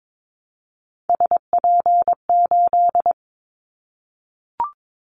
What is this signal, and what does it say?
Morse code, heard as a keyed tone, about 22 wpm, 700 Hz. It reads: SP8
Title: Callsign Prefixes for Top 25 DXCC - 22wpm